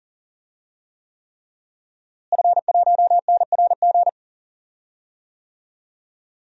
Morse code, heard as a keyed tone, about 40 wpm, 700 Hz. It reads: F1NRG
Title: DXCC-100 Call Signs - 5 Characters - Repeated 3x - 40wpm